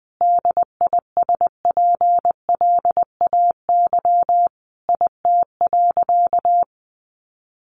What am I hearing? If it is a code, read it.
Morse code, heard as a keyed tone, about 20 wpm, 700 Hz. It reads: DISPLAY IT.
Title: Sentences from Top 1200 Words 20wpm